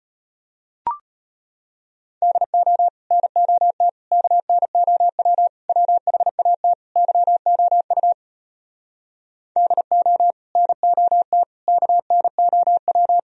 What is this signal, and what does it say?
Morse code, heard as a keyed tone, about 33 wpm, 700 Hz. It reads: DO NOT KNOW WHAT YOU DO NOT KNOW